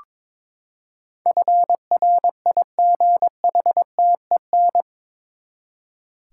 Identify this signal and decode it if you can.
Morse code, heard as a keyed tone, about 22 wpm, 700 Hz. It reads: FRIGHTEN
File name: practice / 22wpm / Eight Letter Words - 22wpm